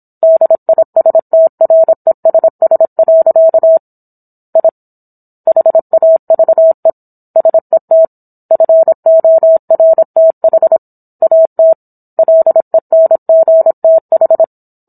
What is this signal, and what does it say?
Morse code, heard as a keyed tone, about 27 wpm, 650 Hz. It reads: DISTRESS. I HAVE SET FORTH AT LENGTH
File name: Book One / 17-Chapter 17 - 26 WPM